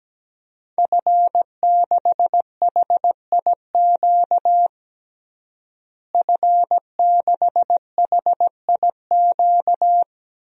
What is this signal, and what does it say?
Morse code, heard as a keyed tone, about 17 wpm, 700 Hz. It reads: F6HIQ F6HIQ
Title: DXCC-100 Call Signs - 5 Characters - Repeated 2x - 17wpm